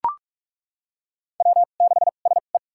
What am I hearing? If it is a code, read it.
Morse code, heard as a keyed tone, about 42 wpm, spaced as for 31 wpm, 700 Hz. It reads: W6SE
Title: USA Call Signs - 2x Character Spacing - 45wpm